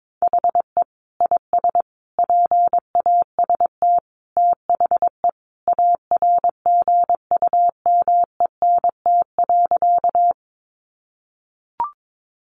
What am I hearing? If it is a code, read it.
Morse code, heard as a keyed tone, about 22 wpm, 700 Hz. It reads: HE IS PAST THE ARGUMENT.
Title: Sentences from Top 1400 Words 22wpm